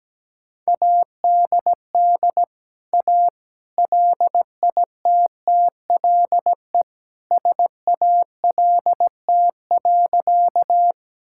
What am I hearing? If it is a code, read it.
Morse code, heard as a keyed tone, about 17 wpm, 700 Hz. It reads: ADD A LITTLE SALT.
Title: Sentences from Top 1000 Words - Review Part 5 - 17wpm